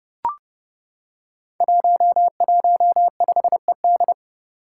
Morse code, heard as a keyed tone, about 30 wpm, 700 Hz. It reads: 115ED
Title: Five Challenging Letter-Number Combinations 30wpm